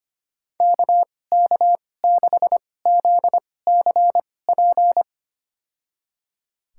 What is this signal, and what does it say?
Morse code, heard as a keyed tone, about 25 wpm, spaced as for 18 wpm, 700 Hz. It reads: KK6ZCP
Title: USA Call Signs - 6 Characters - 2x Character Spacing - 25wpm